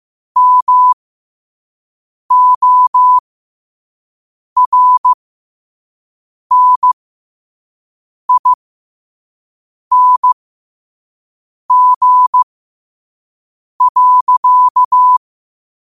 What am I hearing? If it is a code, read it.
Morse code, heard as a keyed tone, about 15 wpm, spaced as for 5 wpm, 1000 Hz. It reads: MORNING.
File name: qotdfiles / 5wpm